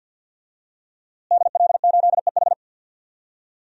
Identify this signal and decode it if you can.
Morse code, heard as a keyed tone, about 53 wpm, 700 Hz. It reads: DL8EH
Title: DXCC-100 Call Signs - 5 Characters - Repeated 2x - 50wpm